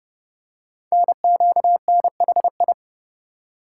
Morse code, heard as a keyed tone, about 29 wpm, 690 Hz. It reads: NQNHI
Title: Code Groups - 5 Letters - 30wpm